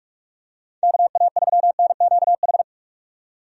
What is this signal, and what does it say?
Morse code, heard as a keyed tone, about 42 wpm, 700 Hz. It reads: KA3NQH
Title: USA Call Signs - 6 Characters - 45wpm